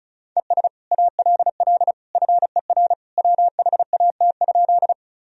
Morse code, heard as a keyed tone, about 36 wpm, 700 Hz. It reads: ES ALL FER WHAT?
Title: Sentences from Top 100 Words and Abbreviations 35wpm